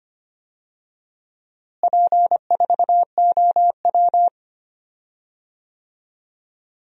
Morse code, heard as a keyed tone, about 25 wpm, 700 Hz. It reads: P4OW
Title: DXCC-100 Call Signs - 4 Characters - Repeated 2x - 25wpm